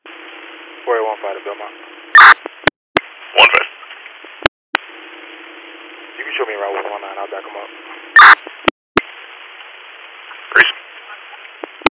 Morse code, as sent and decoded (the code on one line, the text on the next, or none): none